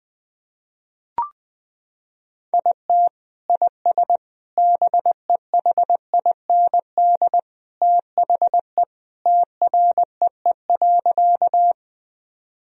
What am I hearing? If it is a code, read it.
Morse code, heard as a keyed tone, about 20 wpm, 700 Hz. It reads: IT IS BEHIND THE TREE.
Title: Sentences from Top 1000 Words - Review Part 1 - 20wpm